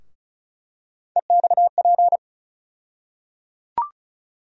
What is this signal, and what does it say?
Morse code, heard as a keyed tone, about 36 wpm, 700 Hz. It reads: EXP
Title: USA Call Sign Postfixes - 3 Letters 35wpm